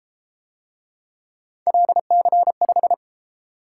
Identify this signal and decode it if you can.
Morse code, heard as a keyed tone, about 31 wpm, 700 Hz. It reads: LC5